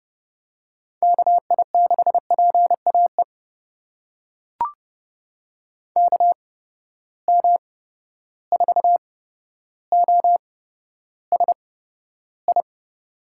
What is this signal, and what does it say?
Morse code, heard as a keyed tone, about 32 wpm, spaced as for 13 wpm, 700 Hz. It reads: KI6PAE KM4OSI
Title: USA Call Signs - 6 Characters - 8x Character Spacing - 30wpm